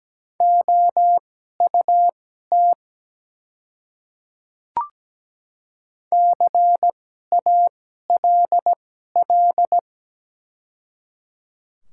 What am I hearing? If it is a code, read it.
Morse code, heard as a keyed tone, about 17 wpm, spaced as for 12 wpm, 700 Hz. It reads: OUT CALL
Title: 2x Character Spacing - Top 100 Words - 17wpm